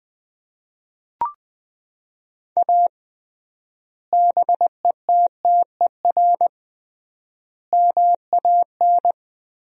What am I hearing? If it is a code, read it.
Morse code, heard as a keyed tone, about 21 wpm, 700 Hz. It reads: A BETTER MAN
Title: 3x Word-Spacing - Sentences from Top 400 Words - 20wpm